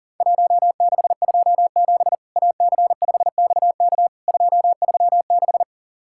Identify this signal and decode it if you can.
Morse code, heard as a keyed tone, about 40 wpm, 700 Hz. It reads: 1627 AC5XK 236
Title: Worked All Europe DX Contest - QTC Exchange - 40wpm